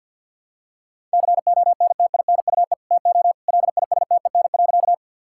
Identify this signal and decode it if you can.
Morse code, heard as a keyed tone, about 50 wpm, 700 Hz. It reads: CONTINUE TO LISTEN.
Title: Sentences from Top 1000 Words - Review Part 6 - 50wpm